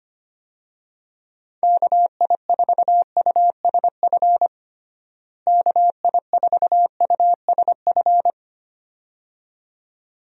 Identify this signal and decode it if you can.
Morse code, heard as a keyed tone, about 25 wpm, 700 Hz. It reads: KI4USF KI4USF
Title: USA Call Signs - 6 Characters - Repeated 2x - 25wpm